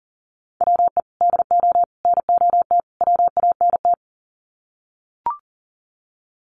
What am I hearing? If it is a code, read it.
Morse code, heard as a keyed tone, about 40 wpm, 700 Hz. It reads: WE DO NOT WANT
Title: Sets of 4 Words 40wpm